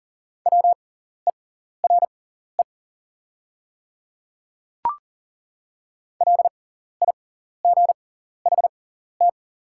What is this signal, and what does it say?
Morse code, heard as a keyed tone, about 40 wpm, spaced as for 14 wpm, 700 Hz. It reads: WERE LIGHT
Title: 6x Character Spacing - Top 200 Words - 40wpm